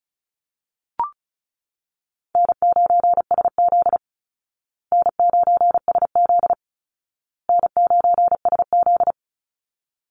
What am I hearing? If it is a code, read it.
Morse code, heard as a keyed tone, about 34 wpm, 700 Hz. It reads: N9SZ N9SZ N9SZ
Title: Easy Call Signs Repeated 3x - 35wpm